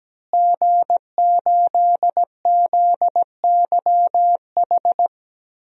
Morse code, heard as a keyed tone, about 17 wpm, 700 Hz. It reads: G8ZYH
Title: DXCC-100 Call Signs - 5 Characters - Repeated 3x - 17wpm